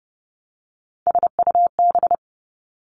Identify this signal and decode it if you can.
Morse code, heard as a keyed tone, about 30 wpm, 700 Hz. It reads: SUB
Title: Three Challenging Letter-Number Combinations 30wpm